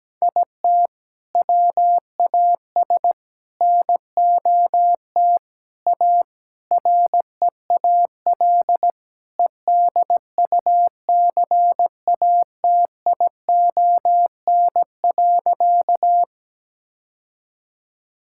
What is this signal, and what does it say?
Morse code, heard as a keyed tone, about 17 wpm, 700 Hz. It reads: IT WAS NOT A REAL EDUCATION.
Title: Sentences from Top 1000 Words - Review Part 6 - 17wpm